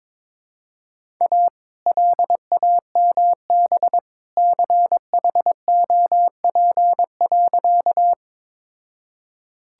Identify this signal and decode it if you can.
Morse code, heard as a keyed tone, about 22 wpm, 700 Hz. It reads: A LAMB CHOP.
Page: morse-code-ninja-practice-22wpm